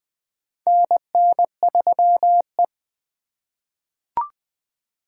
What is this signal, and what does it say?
Morse code, heard as a keyed tone, about 20 wpm, 700 Hz. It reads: NN3E
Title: 2023 POTA Activator Call Signs - 20wpm